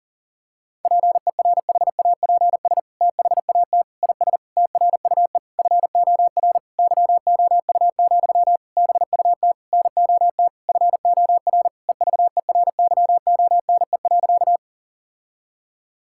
Morse code, heard as a keyed tone, about 40 wpm, 700 Hz. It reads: PERHAPS THAT IS TRUE FOR YOU, BUT NOT FOR EVERYONE.
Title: Sentences from Top 1000 Words - Review Part 5 - 40wpm